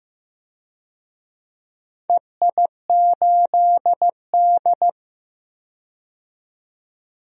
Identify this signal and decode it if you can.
Morse code, heard as a keyed tone, about 15 wpm, 700 Hz. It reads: EI8D
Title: DXCC-100 Call Signs - 4 Characters - 15wpm